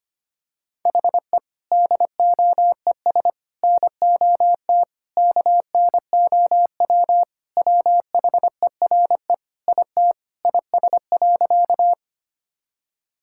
Morse code, heard as a keyed tone, about 25 wpm, 700 Hz. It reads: HE DOES NOT KNOW WHERE IT IS.
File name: practice / 25wpm / Sentences with 4-Word Phrases - 25wpm